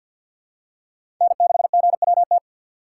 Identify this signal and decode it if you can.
Morse code, heard as a keyed tone, about 50 wpm, 700 Hz. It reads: N6GWT
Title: USA Call Signs - 5 Characters - 2x Character Spacing - 50wpm